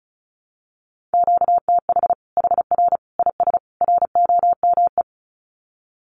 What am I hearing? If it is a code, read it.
Morse code, heard as a keyed tone, about 34 wpm, 700 Hz. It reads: QTH HR IS ROME
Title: Common QSO Phrases 35wpm